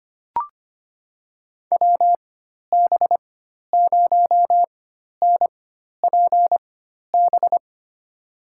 Morse code, heard as a keyed tone, about 25 wpm, spaced as for 12 wpm, 700 Hz. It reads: WB0NPB
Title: USA Call Signs - 6 Characters - 4x Character Spacing - 25wpm